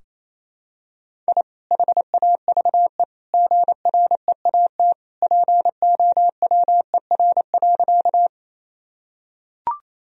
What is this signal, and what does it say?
Morse code, heard as a keyed tone, about 28 wpm, 700 Hz. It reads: I HAVE GREAT POWER.
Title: Sentences from Top 1000 Words - Review Part 2 - 28wpm